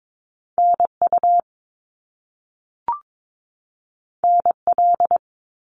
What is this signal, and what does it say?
Morse code, heard as a keyed tone, about 22 wpm, 700 Hz. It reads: NU NL
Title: Canadian Provinces and Territories 22wpm